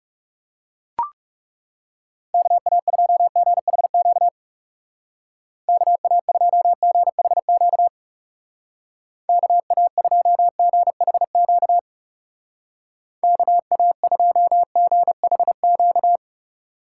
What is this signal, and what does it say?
Morse code, heard as a keyed tone, about 35 wpm, 700 Hz. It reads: KA2GHQ KA2GHQ KA2GHQ KA2GHQ
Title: Speed Racing - USA 2x3 Call Signs 30wpm